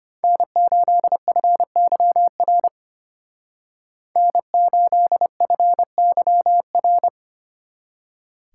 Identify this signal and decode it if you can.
Morse code, heard as a keyed tone, about 30 wpm, 700 Hz. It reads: N8FYR N8FYR
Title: Speed Racing - USA 1x3 Call Signs 25wpm